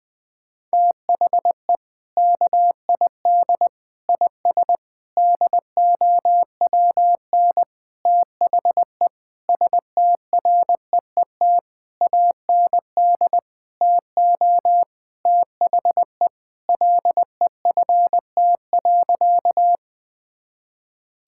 Morse code, heard as a keyed tone, about 20 wpm, 700 Hz. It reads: THE KID IS DOWN THE STREET AND TO THE LEFT.
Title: Sentences from Top 1600 Words 20wpm